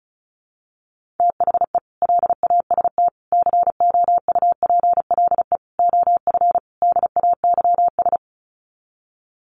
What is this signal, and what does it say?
Morse code, heard as a keyed tone, about 34 wpm, 700 Hz. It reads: THE LAST COUPLE OF DAYS
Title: Sets of 5 Words Encore 35wpm